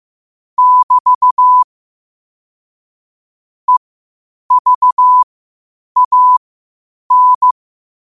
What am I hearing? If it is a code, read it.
Morse code, heard as a keyed tone, about 15 wpm, spaced as for 8 wpm, 1000 Hz. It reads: <BT> EVAN
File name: qotdfiles / 8wpm